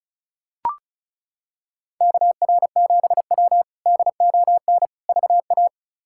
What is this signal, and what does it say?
Morse code, heard as a keyed tone, about 36 wpm, 700 Hz. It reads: KR7W DON VA
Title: K1USN SST QSO Practice - 35wpm